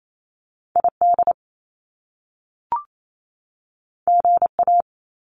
Morse code, heard as a keyed tone, about 28 wpm, 700 Hz. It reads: ID GA
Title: US States 28wpm